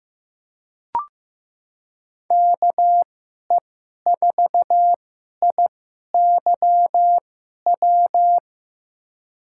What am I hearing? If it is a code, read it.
Morse code, heard as a keyed tone, about 15 wpm, spaced as for 11 wpm, 700 Hz. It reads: KE4IYW
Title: USA Call Signs - 6 Characters - 2x Character Spacing - 15wpm